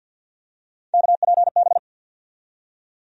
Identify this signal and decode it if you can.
Morse code, heard as a keyed tone, about 47 wpm, 700 Hz. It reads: KPB
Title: USA Call Sign Postfixes - 3 Letters 50wpm